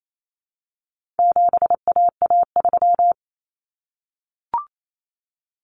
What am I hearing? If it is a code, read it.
Morse code, heard as a keyed tone, about 28 wpm, 700 Hz. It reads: 7AA3
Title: Common Cut Numbers - 4-Digit 28wpm